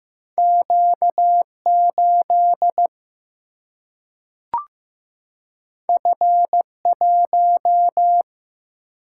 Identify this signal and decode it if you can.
Morse code, heard as a keyed tone, about 15 wpm, 700 Hz. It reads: Q 8 F 1
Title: Single-Letter Single-Number 15wpm